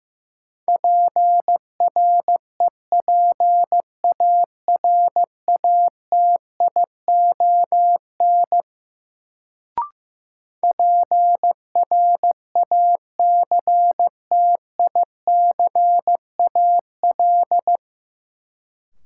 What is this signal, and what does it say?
Morse code, heard as a keyed tone, about 15 wpm, 700 Hz. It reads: PREPARATION PRACTICAL
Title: Sentences from Top 1400 Words - New Word Drill - 15wpm